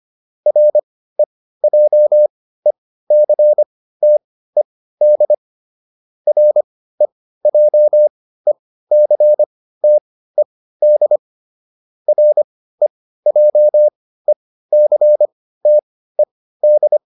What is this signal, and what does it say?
Morse code, heard as a keyed tone, about 25 wpm, spaced as for 15 wpm, 600 Hz. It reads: REJECTED REJECTED REJECTED